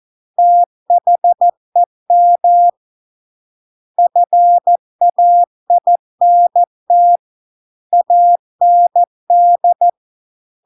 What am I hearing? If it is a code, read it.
Morse code, heard as a keyed tone, about 14 wpm, 700 Hz. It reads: THEM FAINT AND